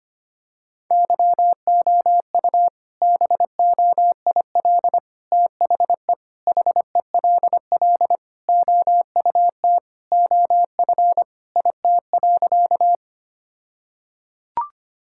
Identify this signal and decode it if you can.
Morse code, heard as a keyed tone, about 25 wpm, 700 Hz. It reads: YOU BOIL THE HELL OUT OF IT.
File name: practice / 25wpm / All-Age Appropriate Jokes - 25wpm